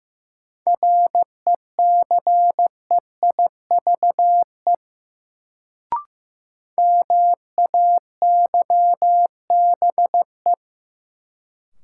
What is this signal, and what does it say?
Morse code, heard as a keyed tone, about 16 wpm, 700 Hz. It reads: RECEIVE MAYBE
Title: Sentences from Top 1000 Words - New Word Drill - 15wpm